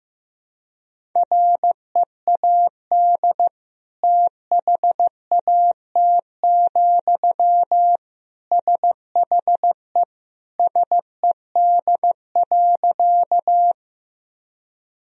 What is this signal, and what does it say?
Morse code, heard as a keyed tone, about 16 wpm, 700 Hz. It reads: READ THAT, SHE SED.
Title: Sentences from Top 200 Words and Abbreviations 15wpm